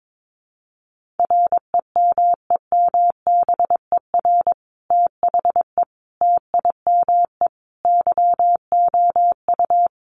Morse code, heard as a keyed tone, about 22 wpm, 700 Hz. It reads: REMEMBER THE TIME YOU
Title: Sentences from Top 1400 Words 22wpm